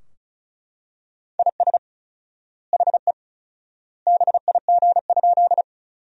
Morse code, heard as a keyed tone, about 34 wpm, 700 Hz. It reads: IS HE BIG?
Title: 4x Word-Spacing - Sentences from Top 200 Words - 35wpm